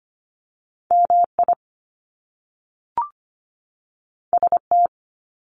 Morse code, heard as a keyed tone, about 25 wpm, 700 Hz. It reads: MI ST
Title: Top Two Letter Combinations in Words 25wpm